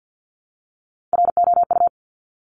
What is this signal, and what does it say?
Morse code, heard as a keyed tone, about 50 wpm, 700 Hz. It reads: ROU
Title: Top Three Letter Combinations in Words 50wpm